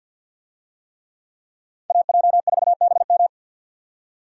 Morse code, heard as a keyed tone, about 51 wpm, 700 Hz. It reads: AJ4BM
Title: USA Call Signs - 5 Characters - 50wpm